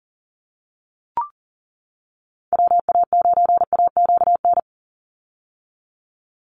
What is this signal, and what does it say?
Morse code, heard as a keyed tone, about 40 wpm, 700 Hz. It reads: WA9AQN
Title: Call Signs - 40wpm